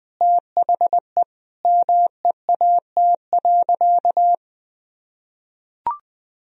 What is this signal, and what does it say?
Morse code, heard as a keyed tone, about 20 wpm, 700 Hz. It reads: THE MEAT.
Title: Sentences from Top 1000 Words - Review Part 3 - 20wpm